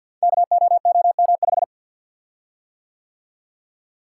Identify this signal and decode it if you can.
Morse code, heard as a keyed tone, about 51 wpm, 700 Hz. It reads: KOOM5
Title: Code Groups - 5 Letter:Numbers - 50wpm